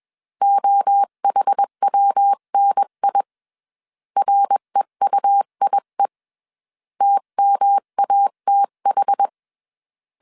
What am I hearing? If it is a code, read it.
Morse code, heard as a keyed tone, about 21 wpm, 800 Hz. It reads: OHWNI REUIE TMATH
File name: CW / DET / 1 3 1500C S B 20wpm 1754